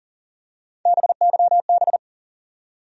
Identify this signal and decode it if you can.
Morse code, heard as a keyed tone, about 38 wpm, 700 Hz. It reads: BYB
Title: USA Call Sign Postfixes - 3 Letters 40wpm